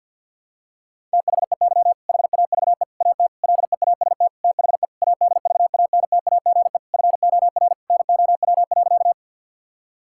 Morse code, heard as a keyed tone, about 47 wpm, 700 Hz. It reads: THEY HAVE AT LEAST THE ADVANTAGE FOR NOW.
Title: Sentences from Top 1000 Words - Review Part 6 - 50wpm